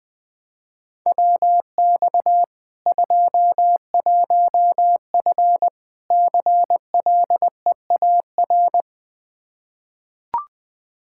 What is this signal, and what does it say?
Morse code, heard as a keyed tone, about 20 wpm, 700 Hz. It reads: WX 21F CLEAR
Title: Practice QSO - Exchange 2 - Info - 20wpm